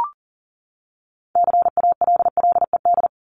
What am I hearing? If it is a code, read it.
Morse code, heard as a keyed tone, about 40 wpm, 700 Hz. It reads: CALLED
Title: ED Suffix - Top 100 Words with Base Words 40wpm